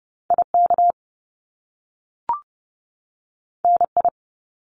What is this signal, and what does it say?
Morse code, heard as a keyed tone, about 30 wpm, 700 Hz. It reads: IK NI
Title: Top Two Letter Combinations in Words 30wpm